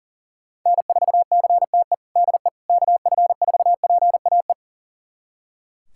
Morse code, heard as a keyed tone, about 40 wpm, 700 Hz. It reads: N4CTE DE KF4PAE
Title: USA Callsign DE Callsign 40wpm